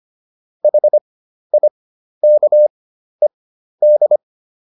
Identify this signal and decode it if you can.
Morse code, heard as a keyed tone, about 25 wpm, spaced as for 12 wpm, 600 Hz. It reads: HIKED